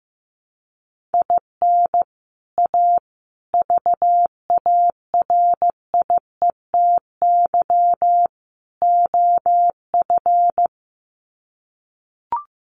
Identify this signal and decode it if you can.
Morse code, heard as a keyed tone, about 15 wpm, 700 Hz. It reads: IN A VARIETY OF
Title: Sets of 4 Words 15wpm